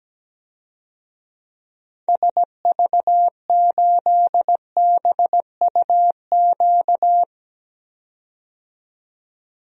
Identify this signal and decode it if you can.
Morse code, heard as a keyed tone, about 17 wpm, 700 Hz. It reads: SV8BUQ
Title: DXCC-100 Call Signs - 6 Characters - 17wpm